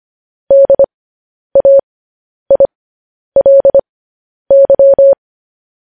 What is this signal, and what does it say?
Morse code, heard as a keyed tone, about 25 wpm, spaced as for 10 wpm, 550 Hz. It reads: DAILY
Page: news headlines in morse code at 10 wpm